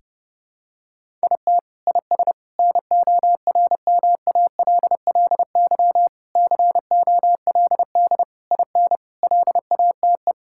Morse code, heard as a keyed tone, about 30 wpm, 700 Hz. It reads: IT IS NORMALLY COLD IN LATE
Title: Sentences from Top 1000 Words - Review Part 6 - 30wpm